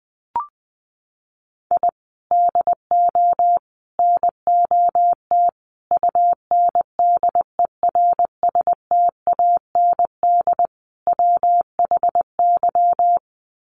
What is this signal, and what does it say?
Morse code, heard as a keyed tone, about 20 wpm, 700 Hz. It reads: I DO NOT UNDERSTAND WHY
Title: Sets of 5 Words Encore 20wpm